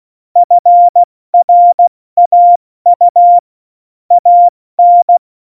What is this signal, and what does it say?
Morse code, heard as a keyed tone, about 16 wpm, 700 Hz. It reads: FRAU AN
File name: Morsekurs / Witze / Witz 27. Oktober 2020, 70 F80